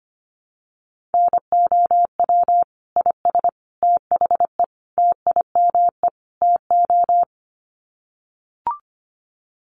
Morse code, heard as a keyed tone, about 25 wpm, 700 Hz. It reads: NOW IS THE TIME TO
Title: Sets of 5 Words Encore2 25wpm